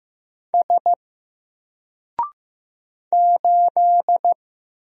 Morse code, heard as a keyed tone, about 15 wpm, 700 Hz. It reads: S 8
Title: Single Character (Letter:Number:Punctuation:Prosign) Alt 15wpm